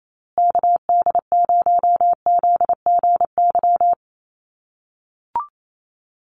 Morse code, heard as a keyed tone, about 28 wpm, 700 Hz. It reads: KD0ZGY
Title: USA Call Signs Repeated 3x 28wpm